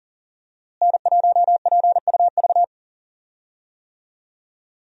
Morse code, heard as a keyed tone, about 39 wpm, 700 Hz. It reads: N1PUV
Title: USA Call Signs - 5 Characters - Repeated 2x - 40wpm